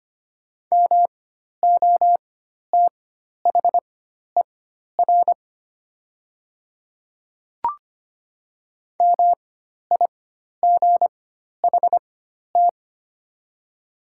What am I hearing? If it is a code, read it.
Morse code, heard as a keyed tone, about 25 wpm, spaced as for 12 wpm, 700 Hz. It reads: MOTHER MIGHT
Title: 4x Character Spacing - Top 300 Words - 25wpm